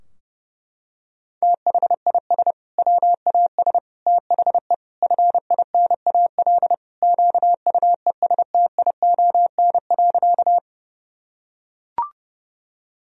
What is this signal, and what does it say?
Morse code, heard as a keyed tone, about 32 wpm, 700 Hz. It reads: THIS WAS THE FINAL QUESTION.
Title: Sentences from Top 1000 Words - Review Part 2 - 30wpm